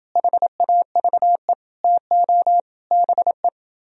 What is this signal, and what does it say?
Morse code, heard as a keyed tone, about 29 wpm, 700 Hz. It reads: HAVE TO BE